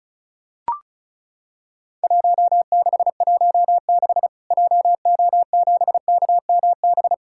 Morse code, heard as a keyed tone, about 36 wpm, 700 Hz. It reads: 1616 JO7KMB
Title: Worked All Europe DX Contest - QTC Exchange - 35wpm